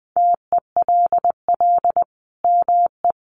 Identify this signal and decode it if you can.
Morse code, heard as a keyed tone, about 20 wpm, 700 Hz. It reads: TELL ME
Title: Sentences from Top 800 Words 20wpm